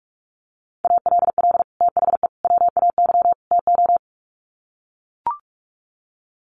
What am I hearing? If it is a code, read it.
Morse code, heard as a keyed tone, about 45 wpm, 700 Hz. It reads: ALL THE WAY TO
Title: Sets of 4 Words 45wpm